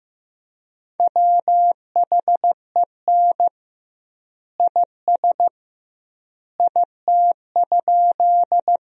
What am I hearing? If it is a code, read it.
Morse code, heard as a keyed tone, about 15 wpm, 700 Hz. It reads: WHEN IS IT?
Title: 2x Word-Spacing - Sentences from Top 100 Words - 15wpm